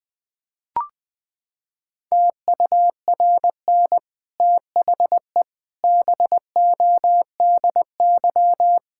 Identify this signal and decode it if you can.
Morse code, heard as a keyed tone, about 20 wpm, 700 Hz. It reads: TURN THE BODY
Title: Sentences from Top 1000 Words - Review Part 6 - 20wpm